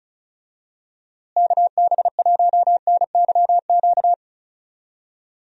Morse code, heard as a keyed tone, about 36 wpm, 700 Hz. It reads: KB1NYQ
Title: USA Call Signs - 6 Characters - Repeated 2x - 35wpm